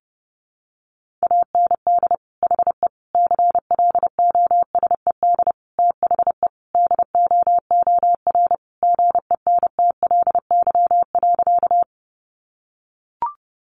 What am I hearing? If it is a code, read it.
Morse code, heard as a keyed tone, about 30 wpm, 700 Hz. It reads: AND HE CLOSED THE DOOR GENTLY.
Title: Sentences from Top 1500 Words 30wpm